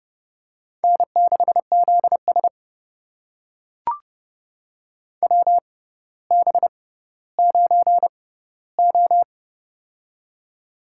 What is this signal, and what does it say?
Morse code, heard as a keyed tone, about 30 wpm, spaced as for 15 wpm, 700 Hz. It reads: N6ZS WB9O
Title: USA Call Signs - 4 Characters - 6x Character Spacing - 30wpm